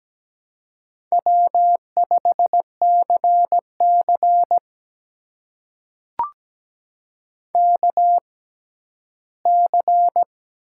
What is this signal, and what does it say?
Morse code, heard as a keyed tone, about 17 wpm, spaced as for 12 wpm, 700 Hz. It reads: W5CC KC
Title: USA Call Signs - 6x Character Spacing - 17wpm